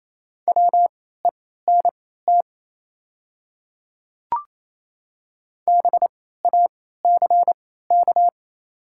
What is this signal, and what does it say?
Morse code, heard as a keyed tone, about 27 wpm, spaced as for 16 wpm, 700 Hz. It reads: WENT BACK
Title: 3x Character Spacing - Top 200 Words - 28wpm